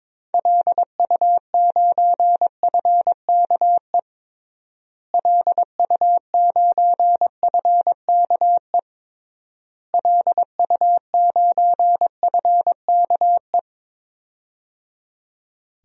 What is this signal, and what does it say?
Morse code, heard as a keyed tone, about 22 wpm, 700 Hz. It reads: LU9FKE LU9FKE LU9FKE
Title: DXCC-100 Call Signs - 6 Characters - Repeated 3x - 22wpm